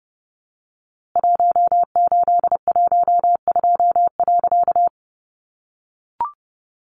Morse code, heard as a keyed tone, about 30 wpm, 700 Hz. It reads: 1812.
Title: A Sound Of Thunder 30wpm